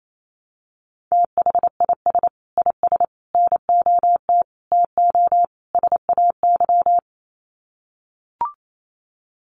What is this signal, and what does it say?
Morse code, heard as a keyed tone, about 28 wpm, 700 Hz. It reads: THIS IS NOT TO SAY
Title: Sets of 5 Words Encore 28wpm